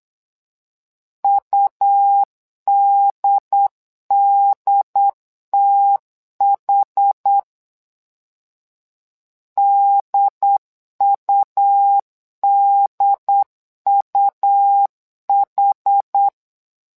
Morse code, heard as a keyed tone, about 8 wpm, 800 Hz. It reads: UDDTH DUDUH
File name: LEZIONI / 2 UHDN / 2 UHD 8wpm 2258